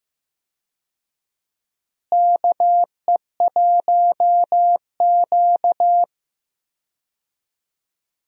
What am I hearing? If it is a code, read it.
Morse code, heard as a keyed tone, about 16 wpm, 700 Hz. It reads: KE1Q
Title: USA Call Signs - 4 Characters - Repeated 2x - 15wpm